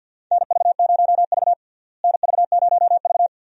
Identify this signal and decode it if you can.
Morse code, heard as a keyed tone, about 51 wpm, 700 Hz. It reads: NV0V NV0V
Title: USA Call Signs - 4 Characters - Repeated 2x - 50wpm